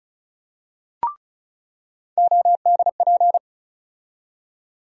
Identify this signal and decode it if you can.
Morse code, heard as a keyed tone, about 36 wpm, 700 Hz. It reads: ODP
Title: USA Call Sign Postfixes - 3 Letters 35wpm